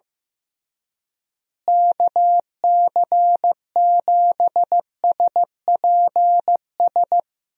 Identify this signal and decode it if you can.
Morse code, heard as a keyed tone, about 15 wpm, 700 Hz. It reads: KC7SPS
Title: USA Call Signs - 6 Characters - Repeated 3x - 15wpm